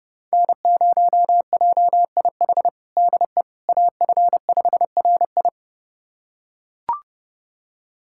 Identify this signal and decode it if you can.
Morse code, heard as a keyed tone, about 30 wpm, 700 Hz. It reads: N0JIH DE AF5RI
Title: USA Callsign DE Callsign - 5 Characters - 30wpm